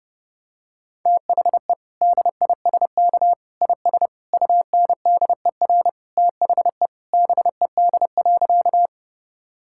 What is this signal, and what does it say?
Morse code, heard as a keyed tone, about 30 wpm, 700 Hz. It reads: THE DISK IS UNDER THE BED.
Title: Sentences from Top 1000 Words - Review Part 6 - 30wpm